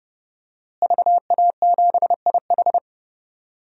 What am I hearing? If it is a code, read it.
Morse code, heard as a keyed tone, about 32 wpm, 700 Hz. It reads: VA7IH
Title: DXCC-100 Call Signs - 5 Characters - Repeated 2x - 30wpm